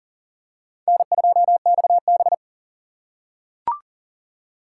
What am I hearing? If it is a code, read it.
Morse code, heard as a keyed tone, about 42 wpm, 700 Hz. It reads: N2XB
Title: USA Call Signs - 4 Characters - 4x Character Spacing - 40wpm